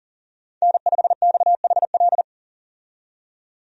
Morse code, heard as a keyed tone, about 40 wpm, 700 Hz. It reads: N5XHL
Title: USA Call Signs - 5 Characters - 40wpm